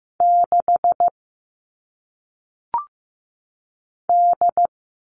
Morse code, heard as a keyed tone, about 15 wpm, 700 Hz. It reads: 6 D
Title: All Cut Numbers - 1-Digit 15wpm